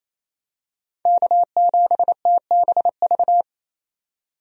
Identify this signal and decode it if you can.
Morse code, heard as a keyed tone, about 28 wpm, 700 Hz. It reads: K7TBV
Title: USA Call Signs - 5 Characters - 28wpm